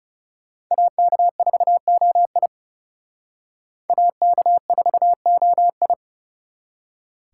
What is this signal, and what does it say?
Morse code, heard as a keyed tone, about 36 wpm, 700 Hz. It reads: AK4OI AK4OI
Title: Speed Racing - USA Call Signs 30wpm